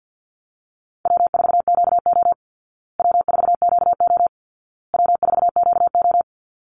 Morse code, heard as a keyed tone, about 48 wpm, 700 Hz. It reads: W4QO W4QO W4QO
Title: Easy Call Signs Repeated 3x - 50wpm